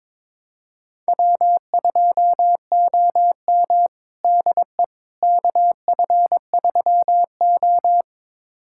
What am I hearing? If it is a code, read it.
Morse code, heard as a keyed tone, about 22 wpm, 700 Hz. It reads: W2OM DE KF3O
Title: USA Callsign DE Callsign - 4 Characters - 22wpm